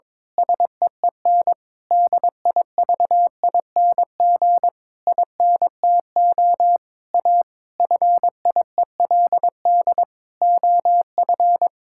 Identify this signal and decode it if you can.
Morse code, heard as a keyed tone, about 22 wpm, 700 Hz. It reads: SEEN DIVING INTO A FIELD OF
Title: Pangram Challenge v2 - 22wpm